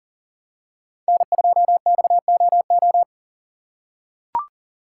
Morse code, heard as a keyed tone, about 40 wpm, 700 Hz. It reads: N2XOO
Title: USA Call Signs - 5 Characters - 2x Character Spacing - 40wpm